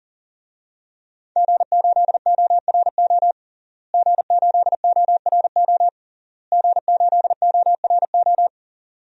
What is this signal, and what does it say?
Morse code, heard as a keyed tone, about 40 wpm, 700 Hz. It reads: G8ORO G8ORO G8ORO
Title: DXCC-100 Call Signs - 5 Characters - Repeated 3x - 40wpm